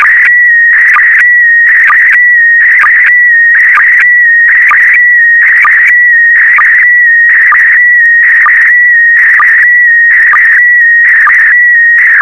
SSTV (slow-scan television), PD 290.
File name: sstv / PD290-VK4MSL